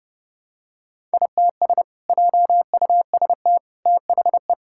Morse code, heard as a keyed tone, about 30 wpm, 700 Hz. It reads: ITS JUST THE
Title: Sentences from Top 1000 Words - Review Part 5 - 30wpm